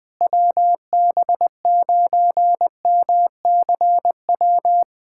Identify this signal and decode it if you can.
Morse code, heard as a keyed tone, about 20 wpm, 700 Hz. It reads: WB9MCW
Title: USA Callsign DE Callsign - 6 Characters - 20wpm